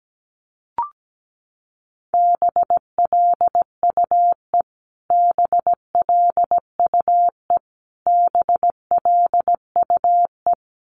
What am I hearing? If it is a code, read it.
Morse code, heard as a keyed tone, about 17 wpm, 700 Hz. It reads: BLUE BLUE BLUE
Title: Top 500 Words Repeated 17wpm